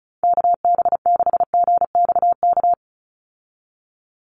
Morse code, heard as a keyed tone, about 34 wpm, 700 Hz. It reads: KB6GXK
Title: USA Call Signs 35wpm